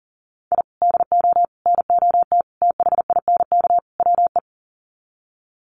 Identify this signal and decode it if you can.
Morse code, heard as a keyed tone, about 40 wpm, 700 Hz. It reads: I DO NOT THINK WE
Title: Sets of 5 Words 40wpm